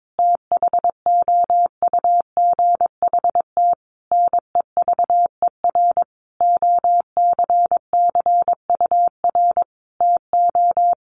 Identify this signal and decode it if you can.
Morse code, heard as a keyed tone, about 22 wpm, 700 Hz. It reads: THOUGHT NEVER OCCUR TO
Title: Sentences from Top 900 Words 22wpm